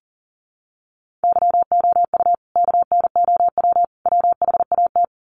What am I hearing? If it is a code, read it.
Morse code, heard as a keyed tone, about 40 wpm, 700 Hz. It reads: YOU KNOW WHAT
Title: Sets of 3 Words 40wpm